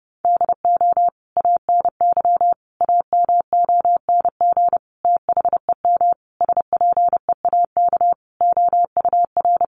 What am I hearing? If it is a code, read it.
Morse code, heard as a keyed tone, about 30 wpm, 700 Hz. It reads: DO ANY AMONG THEM SPEAK OUR
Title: Sentences from Top 1100 Words 30wpm